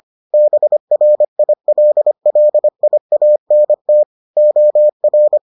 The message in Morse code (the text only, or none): BRILLIANT OR